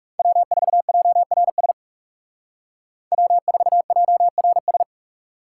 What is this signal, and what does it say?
Morse code, heard as a keyed tone, about 39 wpm, 700 Hz. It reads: W4JRS W4JRS